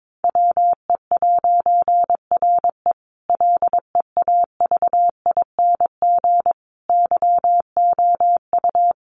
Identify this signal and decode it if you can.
Morse code, heard as a keyed tone, about 22 wpm, 700 Hz. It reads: WE'RE LEAVING YOU
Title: A Sound Of Thunder 22wpm